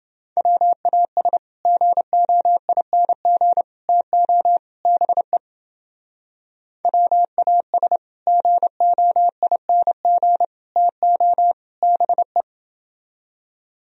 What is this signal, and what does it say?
Morse code, heard as a keyed tone, about 29 wpm, 700 Hz. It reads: WAS GOING TO BE WAS GOING TO BE